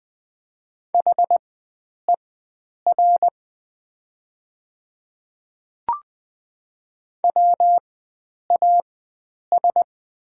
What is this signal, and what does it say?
Morse code, heard as a keyed tone, about 20 wpm, spaced as for 9 wpm, 700 Hz. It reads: HER WAS
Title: 4x Character Spacing - Top 100 Words - 20wpm